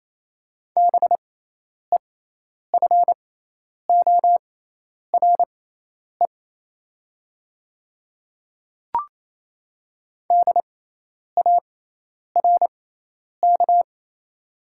Morse code, heard as a keyed tone, about 28 wpm, spaced as for 10 wpm, 700 Hz. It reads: BEFORE DARK
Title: 6x Character Spacing - Top 500 Words - 28wpm